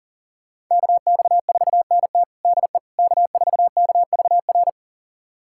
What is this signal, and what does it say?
Morse code, heard as a keyed tone, about 40 wpm, 700 Hz. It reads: KX4NT DE K4KVR
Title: USA Callsign DE Callsign - 5 Characters - 40wpm